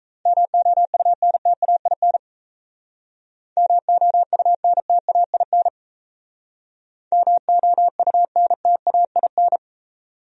Morse code, heard as a keyed tone, about 36 wpm, 700 Hz. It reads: MOUNTAIN MOUNTAIN MOUNTAIN